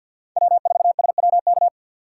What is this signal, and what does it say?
Morse code, heard as a keyed tone, about 49 wpm, 700 Hz. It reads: W4SWK
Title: USA Call Signs - 3x Character Spacing - 50wpm